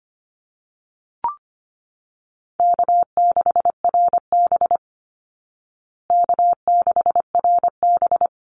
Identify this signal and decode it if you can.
Morse code, heard as a keyed tone, about 25 wpm, 700 Hz. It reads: K6RB K6RB
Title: Easy Call Signs Repeated 3x - 25wpm